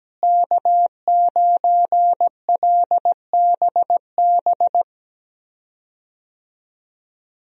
Morse code, heard as a keyed tone, about 17 wpm, 700 Hz. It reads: K9LBB
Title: USA Call Signs - 5 Characters - Repeated 2x - 17wpm